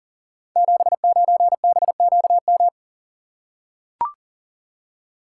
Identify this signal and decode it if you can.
Morse code, heard as a keyed tone, about 40 wpm, 690 Hz. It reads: 79BQM
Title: Code Groups - 5 Letter:Numbers - 40wpm